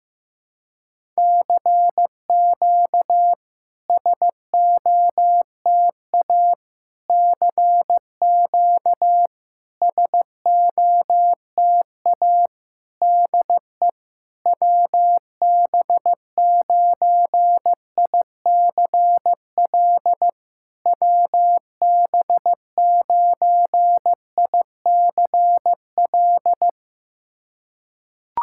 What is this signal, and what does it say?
Morse code, heard as a keyed tone, about 16 wpm, 700 Hz. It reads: CQ SOTA CQ SOTA DE WB9ICL WB9ICL
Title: SOTA QSO Practice 15wpm